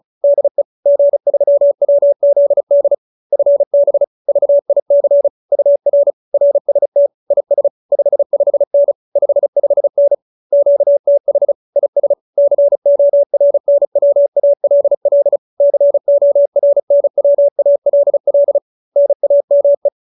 Morse code, heard as a keyed tone, about 34 wpm, 570 Hz. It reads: DE G3WZD FB VIC UR RST IS 55N 55N QTH IS CORNWALL CORNWALL NAME
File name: QSO310 35